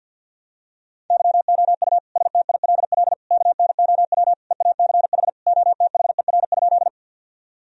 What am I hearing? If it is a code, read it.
Morse code, heard as a keyed tone, about 51 wpm, 700 Hz. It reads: YOU STILL KNOW EACH OTHER?
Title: Sentences from Top 1000 Words - Review Part 1 - 50wpm